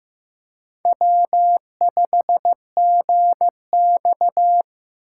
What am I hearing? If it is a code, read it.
Morse code, heard as a keyed tone, about 15 wpm, 700 Hz. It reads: W5GX
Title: USA Call Signs - 4 Characters - 4x Character Spacing - 15wpm